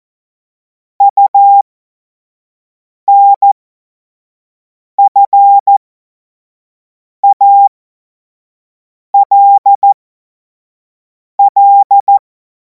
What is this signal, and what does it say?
Morse code, heard as a keyed tone, about 14 wpm, spaced as for 5 wpm, 800 Hz. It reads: UNFALL
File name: Morsekurs / Witze / Witz 03. Februar 2026 25 F70